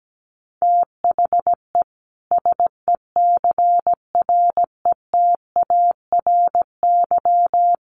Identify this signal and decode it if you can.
Morse code, heard as a keyed tone, about 17 wpm, 700 Hz. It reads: THE SECRETARY
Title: Sentences from Top 1400 Words 17wpm